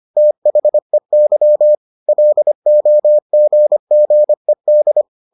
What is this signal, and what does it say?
Morse code, heard as a keyed tone, about 25 wpm, 600 Hz. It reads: THEY LOGGED